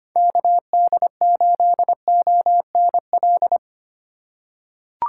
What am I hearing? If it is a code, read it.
Morse code, heard as a keyed tone, about 25 wpm, 700 Hz. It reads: KD8ONL
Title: USA Call Signs - 4x Character Spacing - 25wpm